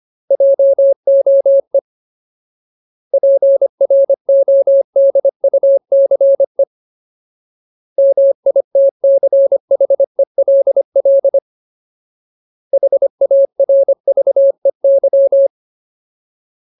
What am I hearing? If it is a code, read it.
Morse code, heard as a keyed tone, about 25 wpm, 550 Hz. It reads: JOE PRODUCE MITCHELL HARVEY
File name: Ultimate Word List / 25WPM Ultimate